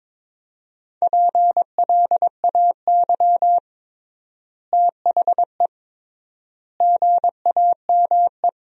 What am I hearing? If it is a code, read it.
Morse code, heard as a keyed tone, about 22 wpm, 700 Hz. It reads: PLAY THE GAME
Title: 3x Word-Spacing - Sentences from Top 500 Words - 22wpm